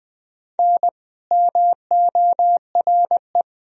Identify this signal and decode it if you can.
Morse code, heard as a keyed tone, about 21 wpm, 700 Hz. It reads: N MORE
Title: Sentences from Top 400 Words and Abbreviations 20wpm